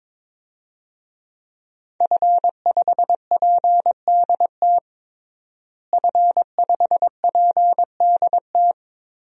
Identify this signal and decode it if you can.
Morse code, heard as a keyed tone, about 22 wpm, 700 Hz. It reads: F5PDT F5PDT
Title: DXCC-100 Call Signs - 5 Characters - Repeated 2x - 22wpm